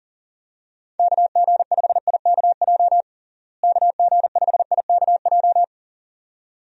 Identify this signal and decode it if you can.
Morse code, heard as a keyed tone, about 40 wpm, 700 Hz. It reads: KG5IKJ KG5IKJ
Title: USA Call Signs - 6 Characters - Repeated 2x - 40wpm